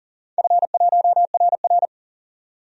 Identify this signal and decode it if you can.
Morse code, heard as a keyed tone, about 40 wpm, 700 Hz. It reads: F1RR
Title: DXCC-100 Call Signs - 4 Characters - 40wpm